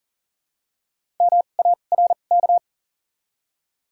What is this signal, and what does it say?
Morse code, heard as a keyed tone, about 40 wpm, spaced as for 29 wpm, 690 Hz. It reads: MARK
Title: 2x Character Spacing - Top 500 Words - 40wpm